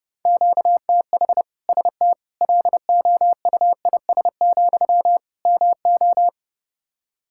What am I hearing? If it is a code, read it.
Morse code, heard as a keyed tone, about 30 wpm, 700 Hz. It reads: QTH ST LOUIS, MO
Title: Practice QSO - Exchange 1 - Beginning - 30wpm